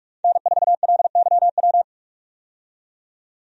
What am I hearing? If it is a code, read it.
Morse code, heard as a keyed tone, about 46 wpm, 700 Hz. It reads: N4LYW
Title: USA Call Signs - 5 Characters - 45wpm